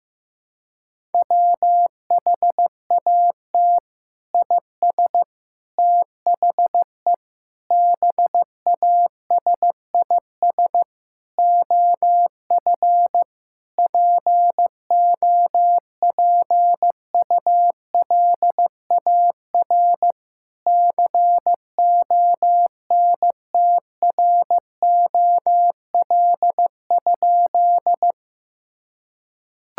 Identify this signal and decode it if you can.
Morse code, heard as a keyed tone, about 16 wpm, 700 Hz. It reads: WHAT IS THE BASIS OF POPULAR CONTROL?
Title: Sentences from Top 1000 Words - Review Part 5 - 15wpm